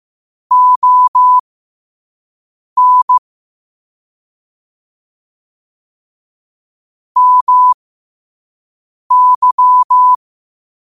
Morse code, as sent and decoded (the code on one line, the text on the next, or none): --- -. / -- -.--
ON MY